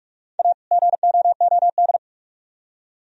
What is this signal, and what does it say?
Morse code, heard as a keyed tone, about 44 wpm, 700 Hz. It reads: A GOOD